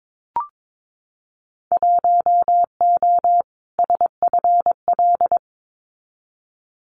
Morse code, heard as a keyed tone, about 22 wpm, 700 Hz. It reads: 1O SFL
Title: Winter Field Day Exchanges Encore 22wpm